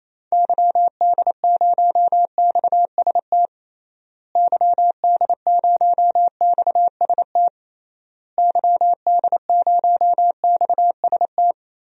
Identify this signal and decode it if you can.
Morse code, heard as a keyed tone, about 28 wpm, 700 Hz. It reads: YD0XST YD0XST YD0XST
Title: DXCC-100 Call Signs - 6 Characters - Repeated 3x - 28wpm